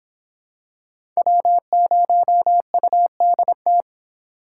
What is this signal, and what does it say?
Morse code, heard as a keyed tone, about 26 wpm, 700 Hz. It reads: W0UDT